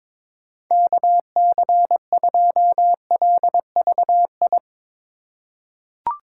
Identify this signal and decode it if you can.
Morse code, heard as a keyed tone, about 22 wpm, 700 Hz. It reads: KC2LVI
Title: USA Call Signs - 6 Characters - 3x Character Spacing - 22wpm